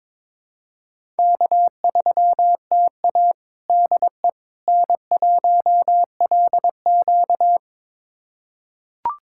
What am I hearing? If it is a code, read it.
Morse code, heard as a keyed tone, about 22 wpm, 700 Hz. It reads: K3TA DE N1LQ
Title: USA Callsign DE Callsign - 4 Characters - 22wpm